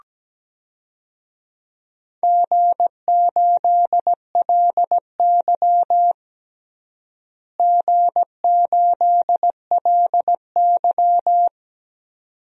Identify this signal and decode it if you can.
Morse code, heard as a keyed tone, about 17 wpm, 700 Hz. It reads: G8LY G8LY
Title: DXCC-100 Call Signs - 4 Characters - Repeated 2x - 17wpm